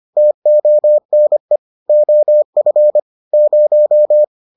Morse code, heard as a keyed tone, about 25 wpm, 600 Hz. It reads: TONE OF 0